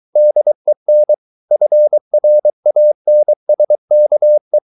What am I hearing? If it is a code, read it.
Morse code, heard as a keyed tone, about 23 wpm, 600 Hz. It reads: DEN FRANSKE
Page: Ferie 23 wpm